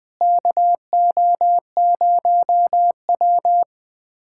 Morse code, heard as a keyed tone, about 21 wpm, 700 Hz. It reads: KO0W
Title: USA Callsign DE Callsign - 4 Characters - 20wpm